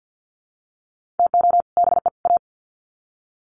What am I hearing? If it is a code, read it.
Morse code, heard as a keyed tone, about 50 wpm, 700 Hz. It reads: TO BE A
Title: Sets of 3 Words 50wpm